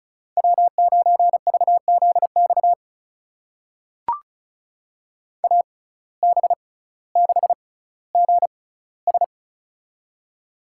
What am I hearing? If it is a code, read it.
Morse code, heard as a keyed tone, about 36 wpm, spaced as for 18 wpm, 700 Hz. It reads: W9VZX AB6GS
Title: USA Call Signs - 6x Character Spacing - 35wpm